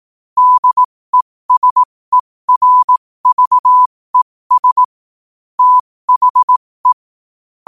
Morse code, heard as a keyed tone, about 18 wpm, spaced as for 15 wpm, 1000 Hz. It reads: DESERVES THE
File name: qotdfiles / 15wpm